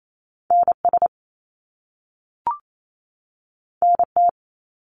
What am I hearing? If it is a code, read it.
Morse code, heard as a keyed tone, about 28 wpm, 700 Hz. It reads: NS NT
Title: Canadian Provinces and Territories 28wpm